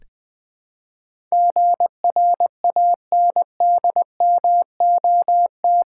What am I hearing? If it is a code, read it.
Morse code, heard as a keyed tone, about 20 wpm, 700 Hz. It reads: GRANDMOT
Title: Build A Word - 4 Letters At A Time - 11-Letter Words - 20wpm